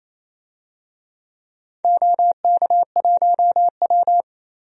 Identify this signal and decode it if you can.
Morse code, heard as a keyed tone, about 28 wpm, 700 Hz. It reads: OK1W
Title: DXCC-100 Call Signs - 4 Characters - 28wpm